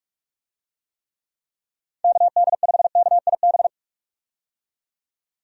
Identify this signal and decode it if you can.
Morse code, heard as a keyed tone, about 43 wpm, 700 Hz. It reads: KD5KIB
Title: USA Call Signs - 6 Characters - 45wpm